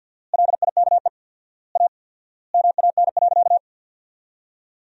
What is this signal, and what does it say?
Morse code, heard as a keyed tone, about 53 wpm, 700 Hz. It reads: LIKE A MAN.
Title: 4x Word-Spacing - Sentences from Top 300 Words - 50wpm